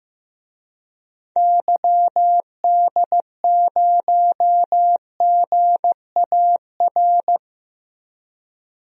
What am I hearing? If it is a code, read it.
Morse code, heard as a keyed tone, about 15 wpm, 700 Hz. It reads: YD0GAR
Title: DXCC-100 Call Signs - 6 Characters - Repeated 2x - 15wpm